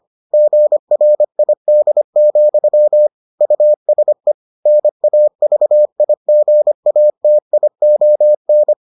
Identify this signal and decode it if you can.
Morse code, heard as a keyed tone, about 25 wpm, 600 Hz. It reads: GRID, USE NAVIGATION